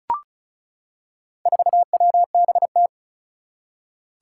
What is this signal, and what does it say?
Morse code, heard as a keyed tone, about 36 wpm, 700 Hz. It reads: 4WBT
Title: Four Challenging Letter-Number Combinations 35wpm